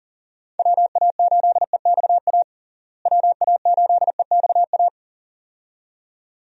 Morse code, heard as a keyed tone, about 40 wpm, 700 Hz. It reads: WA8EXA WA8EXA
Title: USA Call Signs - 6 Characters - Repeated 2x - 40wpm